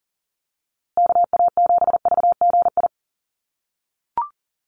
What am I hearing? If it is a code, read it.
Morse code, heard as a keyed tone, about 40 wpm, 700 Hz. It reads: KA7VGI
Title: USA Call Signs Repeated 2x 40wpm